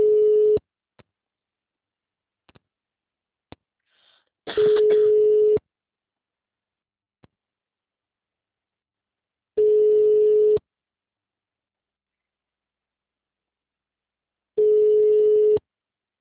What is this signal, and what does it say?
Ringback tone (Europe, 425 Hz).